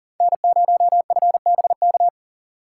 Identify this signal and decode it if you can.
Morse code, heard as a keyed tone, about 40 wpm, 700 Hz. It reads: N0FBK
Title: USA Call Signs - 3x Character Spacing - 40wpm